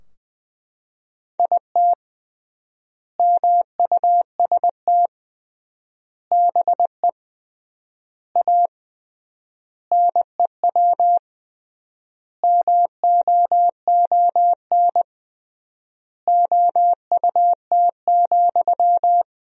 Morse code, heard as a keyed tone, about 20 wpm, 700 Hz. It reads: IT MUST BE A NEW MOON OUT,
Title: 3x Word-Spacing - Sentences from Top 500 Words - 20wpm